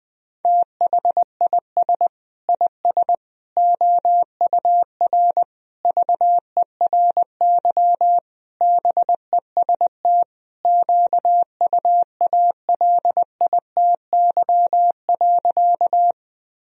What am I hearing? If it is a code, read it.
Morse code, heard as a keyed tone, about 21 wpm, 700 Hz. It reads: THIS IS OUR VERY BEST QUALITY.
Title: Sentences from Top 1000 Words - Review Part 4 - 20wpm